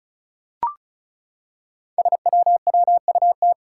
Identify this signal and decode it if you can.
Morse code, heard as a keyed tone, about 36 wpm, 700 Hz. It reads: SWWUT
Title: Five Challenging Letter-Number Combinations 35wpm